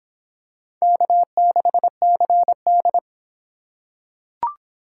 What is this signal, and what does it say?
Morse code, heard as a keyed tone, about 23 wpm, 700 Hz. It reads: K6CD